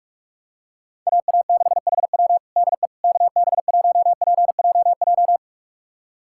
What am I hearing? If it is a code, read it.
Morse code, heard as a keyed tone, about 46 wpm, 700 Hz. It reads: AA6HW DE KB1PJJ
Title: USA Callsign DE Callsign 45wpm